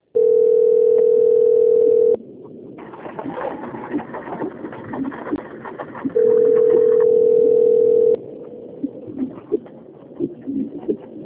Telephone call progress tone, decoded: ringback tone

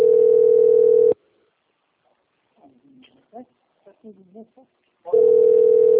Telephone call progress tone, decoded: ringback tone

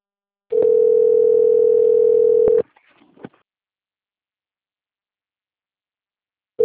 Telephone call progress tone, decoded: ringback tone

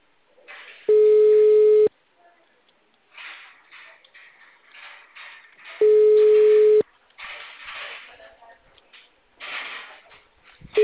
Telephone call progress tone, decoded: ringback tone

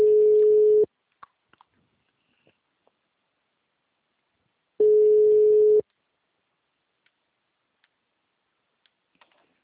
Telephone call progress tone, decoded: ringback tone

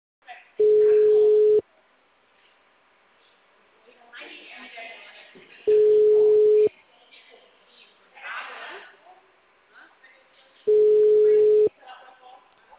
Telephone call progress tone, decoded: ringback tone

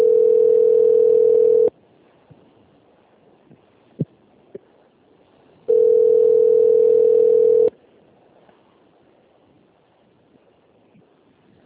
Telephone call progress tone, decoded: ringback tone

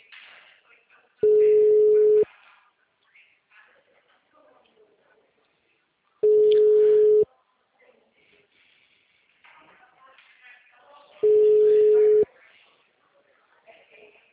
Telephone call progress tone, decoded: ringback tone

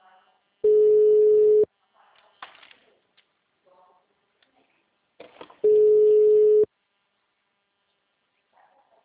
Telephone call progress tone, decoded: ringback tone

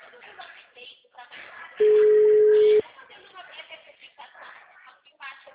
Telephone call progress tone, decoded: ringback tone

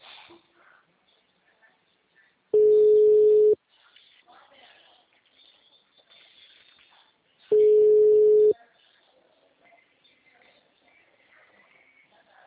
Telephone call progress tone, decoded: ringback tone

